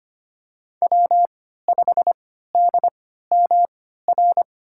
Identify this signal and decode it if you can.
Morse code, heard as a keyed tone, about 25 wpm, spaced as for 14 wpm, 700 Hz. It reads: W5DMR
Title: USA Call Signs - 5 Characters - 3x Character Spacing - 25wpm